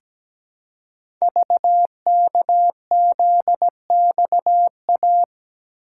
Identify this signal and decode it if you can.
Morse code, heard as a keyed tone, about 17 wpm, 700 Hz. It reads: VKZXA
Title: Code Groups - 5 Letters - 17wpm